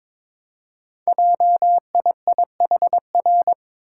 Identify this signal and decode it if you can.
Morse code, heard as a keyed tone, about 22 wpm, 700 Hz. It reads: JIIHR
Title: Code Groups - 5 Letters - 22wpm